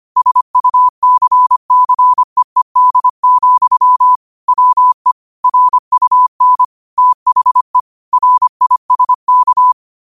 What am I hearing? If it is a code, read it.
Morse code, heard as a keyed tone, about 25 wpm, 1000 Hz. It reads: SUCCEED, WE RUN THE RISK